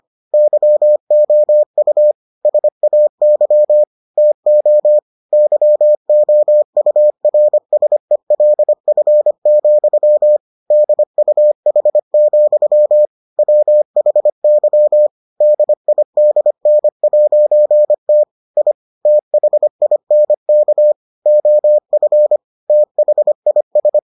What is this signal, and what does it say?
Morse code, heard as a keyed tone, about 25 wpm, 600 Hz. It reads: YOU SAY TO YOURSELF, DUH, WHY DIDN'T I THINK OF THIS